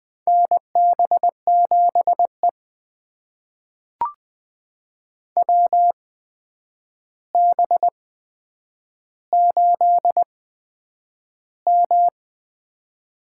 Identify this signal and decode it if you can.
Morse code, heard as a keyed tone, about 20 wpm, spaced as for 9 wpm, 700 Hz. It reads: NB7E WB8M
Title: USA Call Signs - 8x Character Spacing - 20wpm